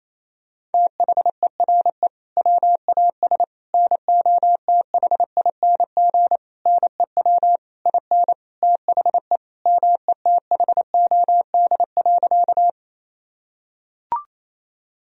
Morse code, heard as a keyed tone, about 28 wpm, 700 Hz. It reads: THERE WAS NOTHING NEW IN THE METHOD.
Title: Sentences from Top 1000 Words - Review Part 3 - 28wpm